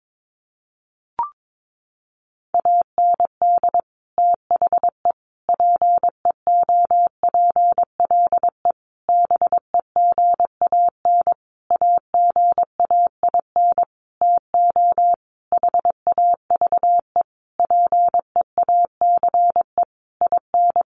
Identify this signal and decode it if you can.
Morse code, heard as a keyed tone, about 22 wpm, 700 Hz. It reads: AND THE PEOPLE BEGAN AGAIN TO HAVE PEACE IN
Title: Sentences from Top 1500 Words 22wpm